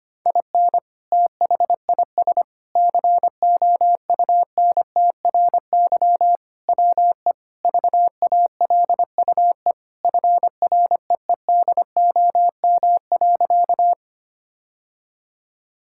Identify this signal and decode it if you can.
Morse code, heard as a keyed tone, about 25 wpm, 700 Hz. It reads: IN THIS COUNTRY WE VALUE FREEDOM.
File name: practice / 25wpm / Sentences with 3-Word Phrases - 25wpm